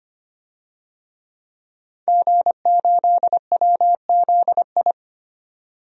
Morse code, heard as a keyed tone, about 25 wpm, 700 Hz. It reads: G8WZI
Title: DXCC-100 Call Signs - 5 Characters - 25wpm